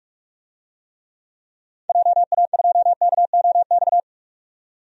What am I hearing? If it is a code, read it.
Morse code, heard as a keyed tone, about 46 wpm, 700 Hz. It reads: JA2KOX
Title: DXCC-100 Call Signs - 6 Characters - 45wpm